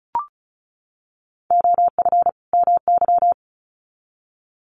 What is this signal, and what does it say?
Morse code, heard as a keyed tone, about 34 wpm, 700 Hz. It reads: OF MY
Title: Sets of 2 Words 35wpm